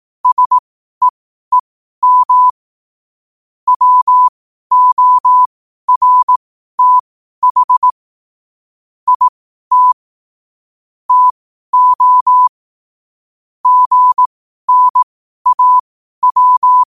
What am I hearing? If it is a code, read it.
Morse code, heard as a keyed tone, about 18 wpm, spaced as for 12 wpm, 1000 Hz. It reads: SEEM WORTH IT TO GNAW
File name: qotdfiles / 12wpm